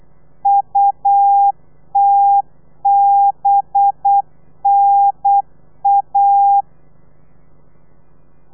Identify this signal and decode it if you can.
Morse code, heard as a keyed tone, about 8 wpm, 800 Hz. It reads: UTBNA
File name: telegrafia / i2tfj / Lez4 35